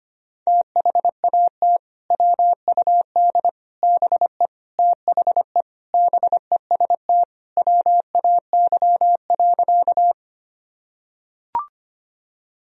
Morse code, heard as a keyed tone, about 25 wpm, 700 Hz. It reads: THAT WUD BE THE BEST WAY.
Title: Sentences from Top 400 Words and Abbreviations 25wpm